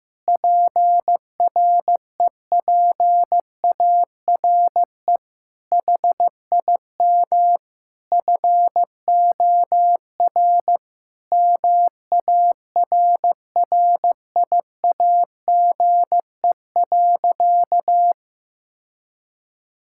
Morse code, heard as a keyed tone, about 15 wpm, 700 Hz. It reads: PREPARE HIM FOR MARRIAGE.
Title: Sentences from Top 1000 Words - Review Part 6 - 15wpm